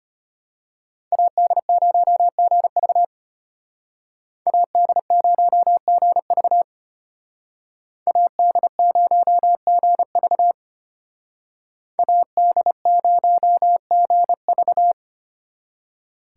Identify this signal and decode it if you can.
Morse code, heard as a keyed tone, about 32 wpm, 700 Hz. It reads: AD0GV AD0GV AD0GV AD0GV